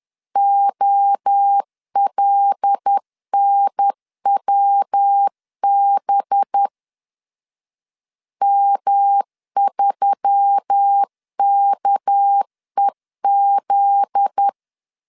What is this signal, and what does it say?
Morse code, heard as a keyed tone, about 10 wpm, 800 Hz. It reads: OLNWB M3KEZ
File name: LEZIONI / 8 YJ45 / 8 Y 10wpm 2615